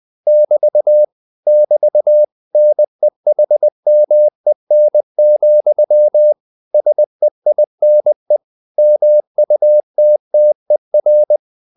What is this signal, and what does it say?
Morse code, heard as a keyed tone, about 20 wpm, 600 Hz. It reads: <BT> <BT> NEHMEN, SEINE MUTTER